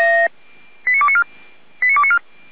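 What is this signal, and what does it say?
Five-tone selective call (ZVEI), heard as quick runs of five tones, tones of about 70 ms. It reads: 89294
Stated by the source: ZVEI five-tone selective call received via SDR, archived by the Feuerwehr Orlamünde fire brigade.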